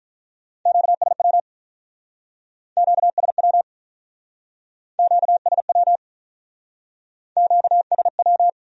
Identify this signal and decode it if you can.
Morse code, heard as a keyed tone, about 43 wpm, 700 Hz. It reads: QSW QSW QSW QSW
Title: Speed-Racing Q-Codes - 35wpm